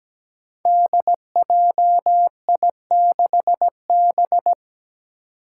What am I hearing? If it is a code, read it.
Morse code, heard as a keyed tone, about 17 wpm, 700 Hz. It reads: DJI6B
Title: Five Challenging Letter-Number Combinations 17wpm